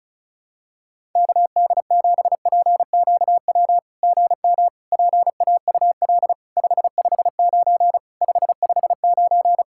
Morse code, heard as a keyed tone, about 33 wpm, 700 Hz. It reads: KD7PQW GM PAUL 559 559
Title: SOTA QSO Practice 35wpm